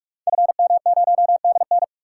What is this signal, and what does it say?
Morse code, heard as a keyed tone, about 44 wpm, 700 Hz. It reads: FM0DN
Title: Code Groups - 5 Letter:Numbers - 45wpm